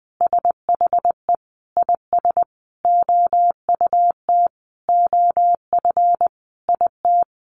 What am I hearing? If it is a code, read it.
Morse code, heard as a keyed tone, about 19 wpm, 700 Hz. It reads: SHE IS OUT OF IT
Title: Sentences from Top 1100 Words 20wpm